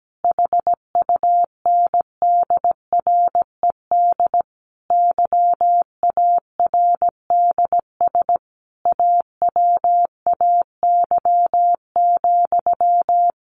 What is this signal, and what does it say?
Morse code, heard as a keyed tone, about 17 wpm, 700 Hz. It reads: HUNDRED YARDS AWAY,
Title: A Sound Of Thunder 17wpm